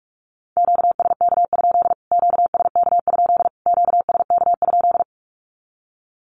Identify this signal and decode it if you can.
Morse code, heard as a keyed tone, about 44 wpm, 700 Hz. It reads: QSK? QSK? QSK?
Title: Q-Codes Repeated 3x 45wpm